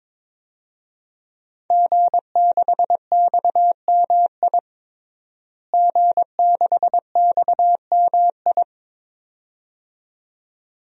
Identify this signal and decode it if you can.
Morse code, heard as a keyed tone, about 22 wpm, 700 Hz. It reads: G6XMI G6XMI
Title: DXCC-100 Call Signs - 5 Characters - Repeated 2x - 22wpm